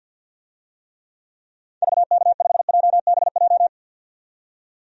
Morse code, heard as a keyed tone, about 52 wpm, 700 Hz. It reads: VK5JBJ
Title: DXCC-100 Call Signs - 6 Characters - 50wpm